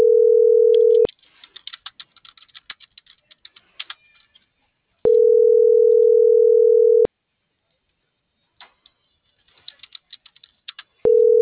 Ringback tone (North America), 440 + 480 Hz.